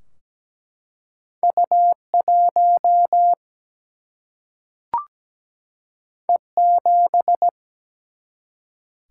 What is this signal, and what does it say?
Morse code, heard as a keyed tone, about 17 wpm, 700 Hz. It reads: U1 E7
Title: Single-Letter Single-Number 17wpm